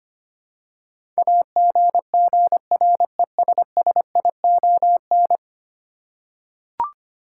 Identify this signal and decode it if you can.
Morse code, heard as a keyed tone, about 25 wpm, 700 Hz. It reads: AGGRESSION
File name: Ten Letter Words - 25wpm